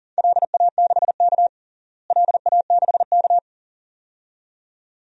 Morse code, heard as a keyed tone, about 40 wpm, 700 Hz. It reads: LA6K LA6K
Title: DXCC-100 Call Signs - 4 Characters - Repeated 2x - 40wpm